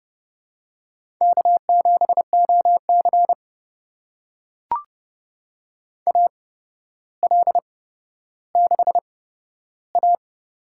Morse code, heard as a keyed tone, about 30 wpm, spaced as for 13 wpm, 700 Hz. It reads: K7OC AL6A
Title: USA Call Signs - 4 Characters - 8x Character Spacing - 30wpm